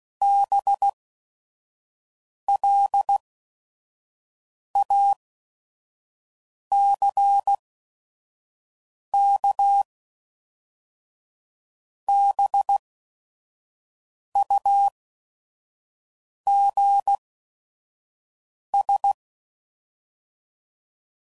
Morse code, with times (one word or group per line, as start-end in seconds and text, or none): 0.2-9.8 BLACK
12.1-19.1 BUGS